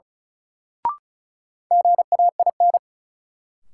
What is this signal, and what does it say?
Morse code, heard as a keyed tone, about 36 wpm, 700 Hz. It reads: GAIN
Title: Sentences from Top 1000 Words - New Word Drill - 35wpm